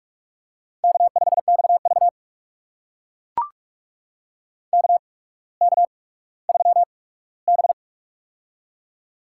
Morse code, heard as a keyed tone, about 44 wpm, spaced as for 19 wpm, 700 Hz. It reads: K5XV KK3B
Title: USA Call Signs - 4 Characters - 8x Character Spacing - 45wpm